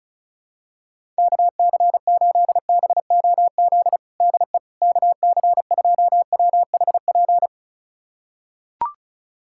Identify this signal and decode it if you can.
Morse code, heard as a keyed tone, about 34 wpm, 700 Hz. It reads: KC8BOZ DE KC2WHP
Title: USA Callsign DE Callsign - 6 Characters - 35wpm